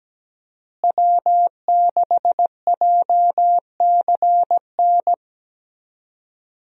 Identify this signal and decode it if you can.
Morse code, heard as a keyed tone, about 17 wpm, 700 Hz. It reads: W6JCN
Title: USA Call Signs - 5 Characters - Repeated 2x - 17wpm